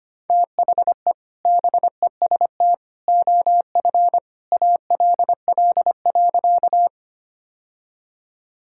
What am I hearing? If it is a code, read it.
Morse code, heard as a keyed tone, about 25 wpm, 700 Hz. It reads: THE BEST OF ALL.
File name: practice / 25wpm / Sentences with 2-Word Phrases - 25wpm